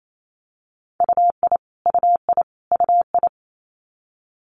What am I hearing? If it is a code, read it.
Morse code, heard as a keyed tone, about 28 wpm, 700 Hz. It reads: UI UI UI
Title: Multi-letter Phonemes Repeated 3x 28wpm